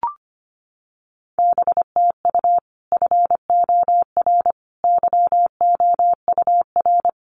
Morse code, heard as a keyed tone, about 25 wpm, 700 Hz. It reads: BTU FOR YOUR
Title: Common QSO Phrases 25wpm